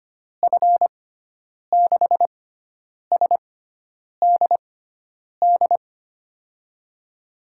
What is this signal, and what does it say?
Morse code, heard as a keyed tone, about 25 wpm, spaced as for 9 wpm, 700 Hz. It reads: F6SDD
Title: USA Call Signs - 6 Characters - 6x Character Spacing - 25wpm